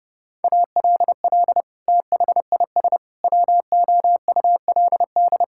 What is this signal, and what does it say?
Morse code, heard as a keyed tone, about 30 wpm, 700 Hz. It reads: ALL THIS WOULD
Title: Sentences from Top 1000 Words - Review Part 6 - 30wpm